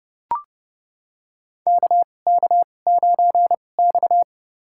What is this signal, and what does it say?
Morse code, heard as a keyed tone, about 30 wpm, spaced as for 22 wpm, 700 Hz. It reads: KK9X
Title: USA Call Signs - 2x Character Spacing - 30wpm